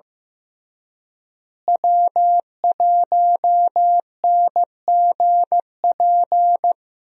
Morse code, heard as a keyed tone, about 15 wpm, 700 Hz. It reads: W1NGP
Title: USA Call Signs - 5 Characters - Repeated 3x - 15wpm